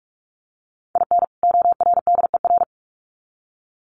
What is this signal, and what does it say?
Morse code, heard as a keyed tone, about 44 wpm, 700 Hz. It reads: IN ORDER
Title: Sets of 2 Words 45wpm